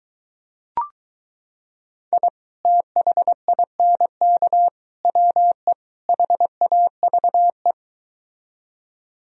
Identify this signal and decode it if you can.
Morse code, heard as a keyed tone, about 23 wpm, 700 Hz. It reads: I THINK WE HAVE